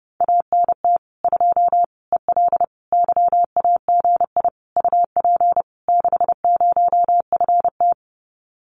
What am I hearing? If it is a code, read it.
Morse code, heard as a keyed tone, about 30 wpm, 700 Hz. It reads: ANT 2 EL YAGI UP 60FT
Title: Common QSO Phrases Encore 30wpm